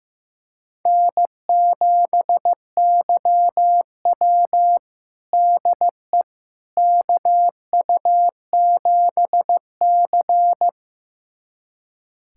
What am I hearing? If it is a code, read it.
Morse code, heard as a keyed tone, about 15 wpm, 700 Hz. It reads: N7YW DE KU7C
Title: USA Callsign DE Callsign - 4 Characters - 15wpm